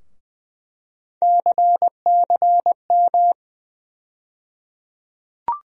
Morse code, heard as a keyed tone, about 20 wpm, 700 Hz. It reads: CCM
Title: USA Call Sign Postfixes - 3 Letters 20wpm